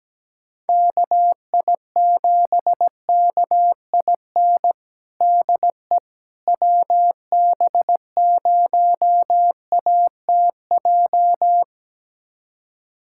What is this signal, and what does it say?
Morse code, heard as a keyed tone, about 17 wpm, 700 Hz. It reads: KI7KIN DE WB0ATJ
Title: USA Callsign DE Callsign 17wpm